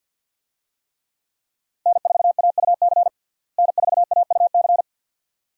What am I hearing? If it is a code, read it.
Morse code, heard as a keyed tone, about 53 wpm, 700 Hz. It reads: N4AUC N4AUC
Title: USA Call Signs - 5 Characters - Repeated 2x - 50wpm